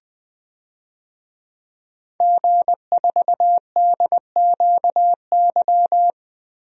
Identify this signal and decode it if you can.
Morse code, heard as a keyed tone, about 20 wpm, 700 Hz. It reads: G4DQY
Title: DXCC-100 Call Signs - 5 Characters - Repeated 2x - 20wpm